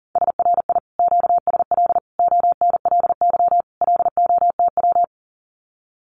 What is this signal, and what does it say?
Morse code, heard as a keyed tone, about 40 wpm, 700 Hz. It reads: SRI QSL ONLY LOTW
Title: Common QSO Phrases Encore 40wpm